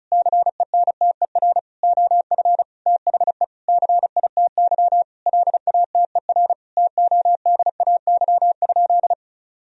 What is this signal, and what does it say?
Morse code, heard as a keyed tone, about 36 wpm, 700 Hz. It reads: CENTER OF THE CITY LATER TODAY?
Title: Sentences from Top 1000 Words - Review Part 2 - 35wpm